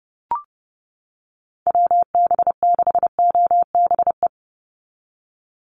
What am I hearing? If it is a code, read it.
Morse code, heard as a keyed tone, about 30 wpm, 700 Hz. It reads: WB6OBE
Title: USA Call Signs 30wpm